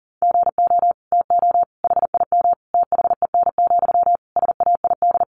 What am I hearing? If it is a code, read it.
Morse code, heard as a keyed tone, about 40 wpm, 700 Hz. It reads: GO TO HIM THEN, SAID
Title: Sentences from Top 1100 Words 40wpm